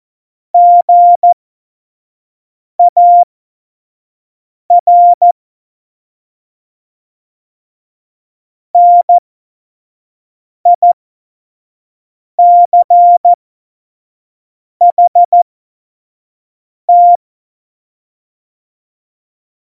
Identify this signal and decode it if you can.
Morse code, heard as a keyed tone, about 14 wpm, spaced as for 5 wpm, 700 Hz. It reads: GAR NICHT
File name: Morsekurs / Witze / Witz 27 Juni 17, Tempo 25 F70